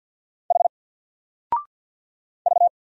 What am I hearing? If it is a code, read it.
Morse code, heard as a keyed tone, about 48 wpm, 700 Hz. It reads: H V
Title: Single Character (Letter:Number:Punctuation:Prosign) Alt 50wpm